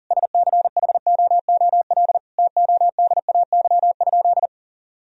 Morse code, heard as a keyed tone, about 40 wpm, 700 Hz. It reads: SCHOOL TODAY?
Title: Sentences from Top 1000 Words - Review Part 1 - 40wpm